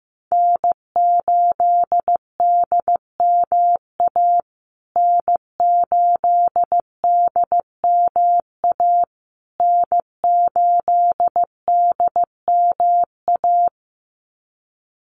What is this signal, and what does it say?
Morse code, heard as a keyed tone, about 15 wpm, 700 Hz. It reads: N8DMA N8DMA N8DMA
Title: USA Call Signs Repeated 3x 15wpm